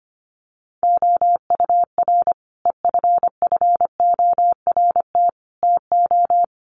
Morse code, heard as a keyed tone, about 25 wpm, 700 Hz. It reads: OUR EFFORT TO
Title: Sentences from Top 1500 Words 25wpm